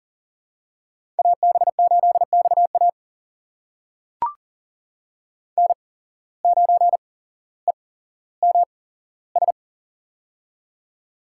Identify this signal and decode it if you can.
Morse code, heard as a keyed tone, about 40 wpm, spaced as for 17 wpm, 700 Hz. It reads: AB8XA N9EMS
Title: USA Call Signs - 8x Character Spacing - 40wpm